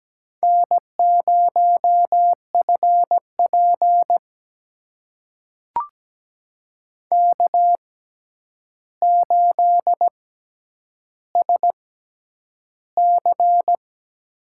Morse code, heard as a keyed tone, about 17 wpm, spaced as for 9 wpm, 700 Hz. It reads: N0FP K8SC
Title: USA Call Signs - 4 Characters - 6x Character Spacing - 17wpm